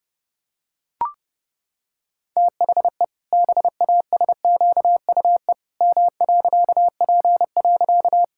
Morse code, heard as a keyed tone, about 30 wpm, 700 Hz. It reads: THE BASQUE M.P.
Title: Pangram Challenge v2 - 30wpm